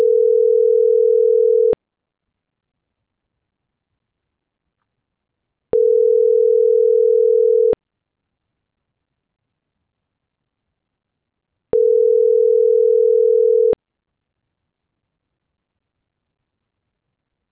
Ringback tone (North America), 440 + 480 Hz.
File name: CENTRAL GENERAL / out yarit.rdz